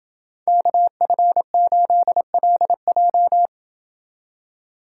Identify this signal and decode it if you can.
Morse code, heard as a keyed tone, about 29 wpm, 700 Hz. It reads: KF8LJ